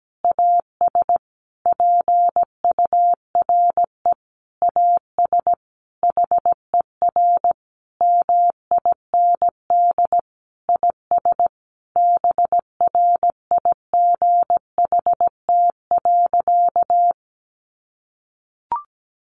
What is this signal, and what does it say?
Morse code, heard as a keyed tone, about 17 wpm, 700 Hz. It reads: AS PURE AS HER MIND IS BRIGHT.
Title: Sentences from Top 1500 Words 17wpm